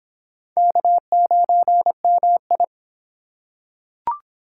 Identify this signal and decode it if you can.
Morse code, heard as a keyed tone, about 25 wpm, 700 Hz. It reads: K9MI